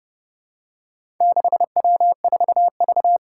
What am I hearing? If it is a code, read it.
Morse code, heard as a keyed tone, about 30 wpm, 700 Hz. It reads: 6W4V
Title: Four Challenging Letter-Number Combinations 30wpm